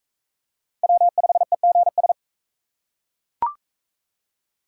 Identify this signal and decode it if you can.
Morse code, heard as a keyed tone, about 40 wpm, 700 Hz. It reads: W5EGS